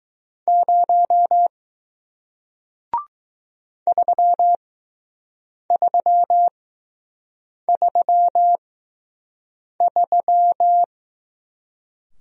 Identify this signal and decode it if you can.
Morse code, heard as a keyed tone, about 20 wpm, spaced as for 6 wpm, 700 Hz. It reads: 0 3333